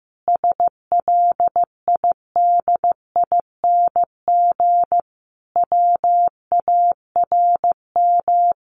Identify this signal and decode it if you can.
Morse code, heard as a keyed tone, about 15 wpm, 700 Hz. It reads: SLIDING WARM
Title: A Sound Of Thunder 15wpm